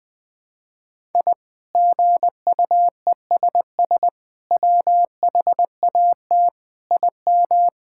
Morse code, heard as a keyed tone, about 20 wpm, 700 Hz. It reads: I GUESS WHAT IM
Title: Sets of 8 Words - 20wpm